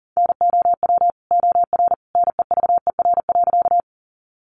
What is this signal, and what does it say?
Morse code, heard as a keyed tone, about 40 wpm, 700 Hz. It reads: NOW OR NEVER.
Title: Sentences from Top 300 Words 40wpm